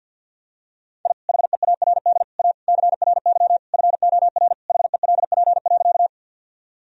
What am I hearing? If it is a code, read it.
Morse code, heard as a keyed tone, about 46 wpm, 700 Hz. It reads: I HEARD A CRY FOR HELP.
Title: Sentences from Top 1000 Words - Review Part 2 - 50wpm